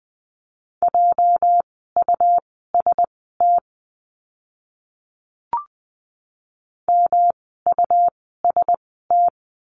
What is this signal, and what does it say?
Morse code, heard as a keyed tone, about 20 wpm, spaced as for 14 wpm, 700 Hz. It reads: JUST MUST
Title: 237 Lesson 38f - Letter q - Words 20wpm